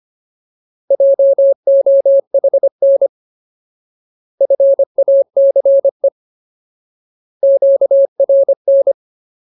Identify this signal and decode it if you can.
Morse code, heard as a keyed tone, about 25 wpm, 550 Hz. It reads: JOHN FACE QRN
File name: Ultimate Word List / 25WPM Ultimate